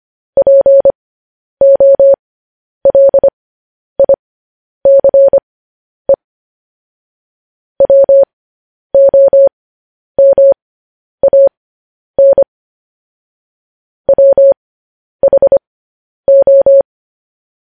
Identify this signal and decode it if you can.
Morse code, heard as a keyed tone, about 25 wpm, spaced as for 10 wpm, 550 Hz. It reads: POLICE WOMAN WHO
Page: news headlines in morse code at 10 wpm